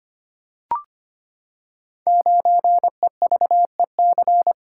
Morse code, heard as a keyed tone, about 25 wpm, 700 Hz. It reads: 9EVEC
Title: Code Groups - 5 Letter:Numbers - 25wpm